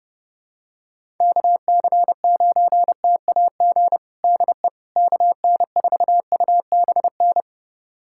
Morse code, heard as a keyed tone, about 30 wpm, 700 Hz. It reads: KC9TAG DE KN4UBN
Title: USA Callsign DE Callsign - 6 Characters - 30wpm